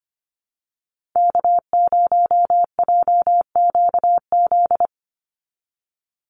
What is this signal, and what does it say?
Morse code, heard as a keyed tone, about 25 wpm, 700 Hz. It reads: K0JQZ
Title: NA SOTA Activator Callsigns 25wpm